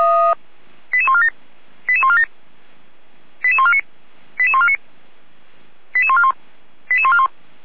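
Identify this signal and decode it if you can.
Five-tone selective call (ZVEI), heard as quick runs of five tones, tones of about 70 ms. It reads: 88147 88147 88149 88149 88141 88141
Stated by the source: ZVEI five-tone selective call 88397 received by SDR.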